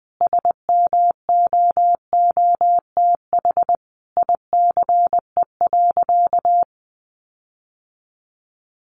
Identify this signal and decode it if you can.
Morse code, heard as a keyed tone, about 20 wpm, 700 Hz. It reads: SMOOTH ICE.
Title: Sentences from Top 1400 Words 20wpm